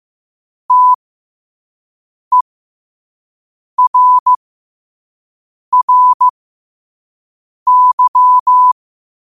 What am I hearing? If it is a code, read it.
Morse code, heard as a keyed tone, about 15 wpm, spaced as for 5 wpm, 1000 Hz. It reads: TERRY
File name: qotdfiles / 5wpm